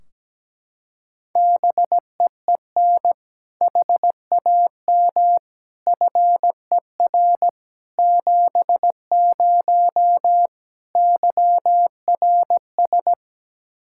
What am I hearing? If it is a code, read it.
Morse code, heard as a keyed tone, about 17 wpm, 700 Hz. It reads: BEEN HAM FER 70 YRS
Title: Practice QSO - Exchange 3 - More Info - 17wpm